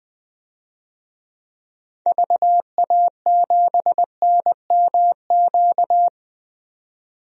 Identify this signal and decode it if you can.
Morse code, heard as a keyed tone, about 20 wpm, 700 Hz. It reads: VA7NMQ
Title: DXCC-100 Call Signs - 6 Characters - 20wpm